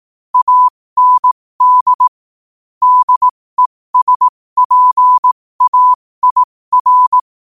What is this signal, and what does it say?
Morse code, heard as a keyed tone, about 18 wpm, spaced as for 15 wpm, 1000 Hz. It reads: AND DESPAIR
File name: qotdfiles / 15wpm